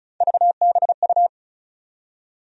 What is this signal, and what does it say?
Morse code, heard as a keyed tone, about 34 wpm, 700 Hz. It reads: VBU
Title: USA Call Sign Postfixes - 3 Letters 35wpm